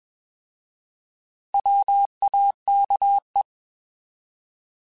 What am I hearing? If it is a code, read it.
Morse code, heard as a keyed tone, about 21 wpm, 800 Hz. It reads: WAKE
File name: Alphabetical / Alphabetical mp3s / Ws @20